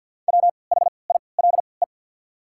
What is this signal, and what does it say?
Morse code, heard as a keyed tone, about 47 wpm, spaced as for 28 wpm, 700 Hz. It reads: WHILE
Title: 3x Character Spacing - Top 300 Words - 50wpm